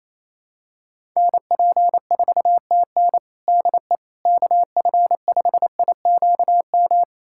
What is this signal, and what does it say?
Morse code, heard as a keyed tone, about 28 wpm, 700 Hz. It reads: NP4TN DE KF5IQM
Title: USA Callsign DE Callsign 28wpm